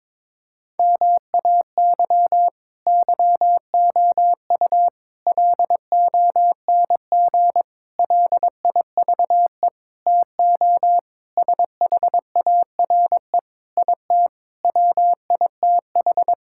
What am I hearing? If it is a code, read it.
Morse code, heard as a keyed tone, about 22 wpm, 700 Hz. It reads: MAY YOU LONG LIVE TO SHARE IT WITH
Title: Sentences from Top 1000 Words - Review Part 6 - 22wpm